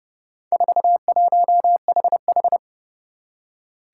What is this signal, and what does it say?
Morse code, heard as a keyed tone, about 30 wpm, 690 Hz. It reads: 41HH
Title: Four Challenging Letter-Number Combinations 30wpm